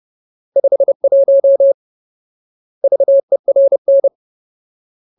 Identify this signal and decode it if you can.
Morse code, heard as a keyed tone, about 30 wpm, 550 Hz. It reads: 51 VERN
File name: Ultimate Word List / 30WPM Ultimate